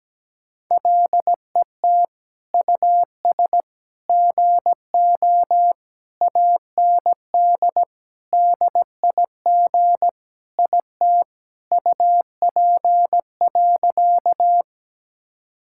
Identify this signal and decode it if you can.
Morse code, heard as a keyed tone, about 17 wpm, 700 Hz. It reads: LET US GO AND DIG IT UP.
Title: Sentences from Top 1600 Words 17wpm